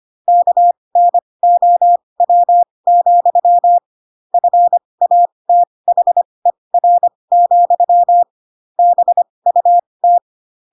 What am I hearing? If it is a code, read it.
Morse code, heard as a keyed tone, about 25 wpm, spaced as for 20 wpm, 700 Hz. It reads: KNOW, FATHER, BUT
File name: Pinocchio / Pinocchio ch7 15 25 ik1hge